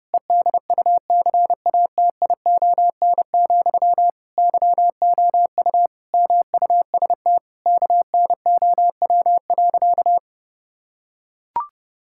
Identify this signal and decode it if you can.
Morse code, heard as a keyed tone, about 30 wpm, 700 Hz. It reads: EDUCATION, YOU MUST KNOW.
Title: Sentences from Top 1000 Words - Review Part 6 - 30wpm